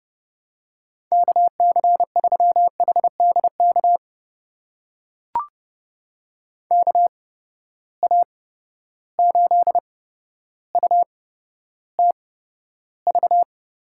Morse code, heard as a keyed tone, about 30 wpm, spaced as for 13 wpm, 700 Hz. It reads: KC3HDK KA8UTV
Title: USA Call Signs - 6 Characters - 8x Character Spacing - 30wpm